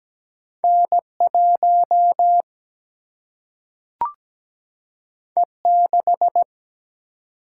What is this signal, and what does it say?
Morse code, heard as a keyed tone, about 17 wpm, 700 Hz. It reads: N1 E6
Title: Single-Letter Single-Number 17wpm